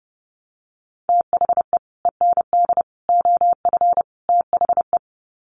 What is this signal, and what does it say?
Morse code, heard as a keyed tone, about 30 wpm, 700 Hz. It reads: THE END OF THE
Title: Sets of 4 Words 30wpm